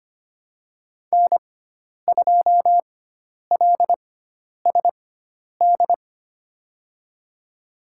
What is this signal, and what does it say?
Morse code, heard as a keyed tone, about 25 wpm, spaced as for 10 wpm, 700 Hz. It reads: N2LSD
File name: Lesson 16g - Letter c - Call Signs 25wpm